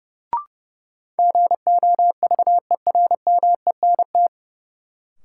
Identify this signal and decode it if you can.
Morse code, heard as a keyed tone, about 30 wpm, 700 Hz. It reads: GOVERMENT
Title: Sentences from Top 1300 Words - New Word Drill Alt - 30wpm